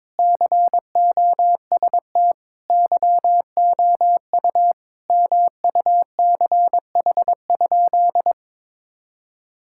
Morse code, heard as a keyed tone, about 22 wpm, 700 Hz. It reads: COST YOU MUCH?
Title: Sentences from Top 1000 Words - Review Part 6 - 22wpm